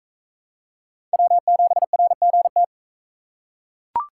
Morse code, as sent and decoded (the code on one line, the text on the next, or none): .-- --... .-. --. -
W7RGT